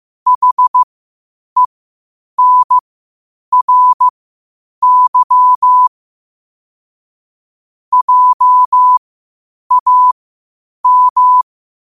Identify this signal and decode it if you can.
Morse code, heard as a keyed tone, about 15 wpm, spaced as for 8 wpm, 1000 Hz. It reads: HENRY JAM
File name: qotdfiles / 8wpm